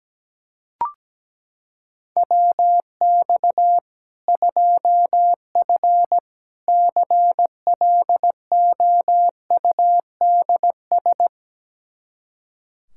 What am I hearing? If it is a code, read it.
Morse code, heard as a keyed tone, about 17 wpm, 700 Hz. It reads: WX 2F CLOUDS
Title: Practice QSO - Exchange 2 - Info - 17wpm